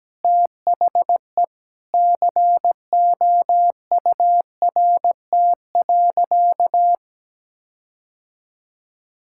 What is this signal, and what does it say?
Morse code, heard as a keyed tone, about 17 wpm, 700 Hz. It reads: THE COURT.
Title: Sentences from Top 1600 Words 17wpm